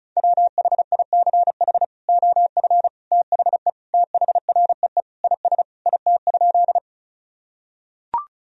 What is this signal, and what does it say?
Morse code, heard as a keyed tone, about 36 wpm, 700 Hz. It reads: WHICH OF THE THREE IS IT?
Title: Sentences from Top 1000 Words - Review Part 1 - 35wpm